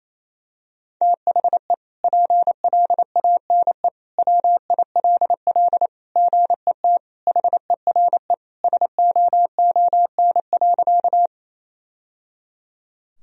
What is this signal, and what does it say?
Morse code, heard as a keyed tone, about 28 wpm, 700 Hz. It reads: THE PLANE WILL GET HERE SOON.
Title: Sentences from Top 1000 Words - Review Part 3 - 28wpm